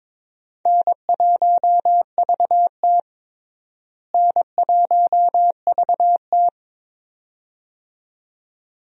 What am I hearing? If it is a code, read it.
Morse code, heard as a keyed tone, about 22 wpm, 700 Hz. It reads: N1VT N1VT
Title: USA Call Signs - 4 Characters - Repeated 2x - 22wpm